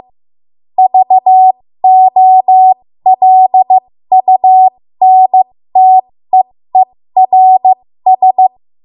Morse code, heard as a keyed tone, about 15 wpm, spaced as for 13 wpm, 750 Hz. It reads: VOLUNTEERS
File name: Morse / 13WPM